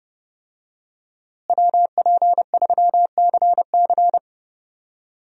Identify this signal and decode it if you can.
Morse code, heard as a keyed tone, about 30 wpm, 700 Hz. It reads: WP3CC
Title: USA Call Signs - 5 Characters - Repeated 2x - 30wpm